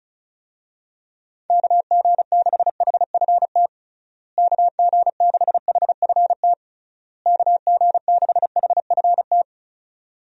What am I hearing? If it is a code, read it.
Morse code, heard as a keyed tone, about 36 wpm, 700 Hz. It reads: KG6HFT KG6HFT KG6HFT
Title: USA Call Signs - 6 Characters - Repeated 3x - 35wpm